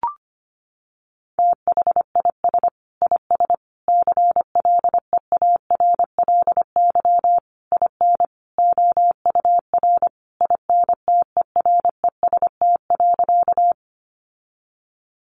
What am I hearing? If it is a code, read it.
Morse code, heard as a keyed tone, about 25 wpm, 700 Hz. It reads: THIS IS CLEARLY IN OUR INTEREST.
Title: Sentences from Top 1300 Words 25wpm